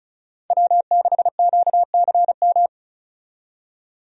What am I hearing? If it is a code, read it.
Morse code, heard as a keyed tone, about 34 wpm, 700 Hz. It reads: W6QCM
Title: USA Call Signs - 5 Characters - 35wpm